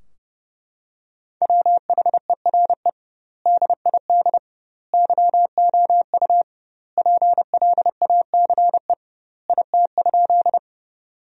Morse code, heard as a keyed tone, about 30 wpm, 700 Hz. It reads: WHERE DID YOU PLACE IT?
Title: 2x Word-Spacing - Sentences from Top 200 Words - 30wpm